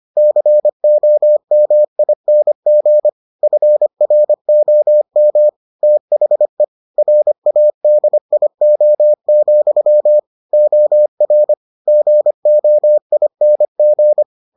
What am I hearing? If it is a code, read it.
Morse code, heard as a keyed tone, about 25 wpm, 600 Hz. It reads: COMING FROM THE RADIO, OR GOING